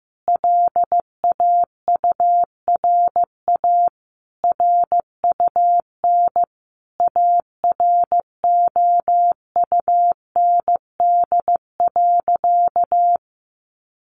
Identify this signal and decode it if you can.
Morse code, heard as a keyed tone, about 15 wpm, 700 Hz. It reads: LAURA RUN AROUND.
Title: Additional Easy Phrases 15wpm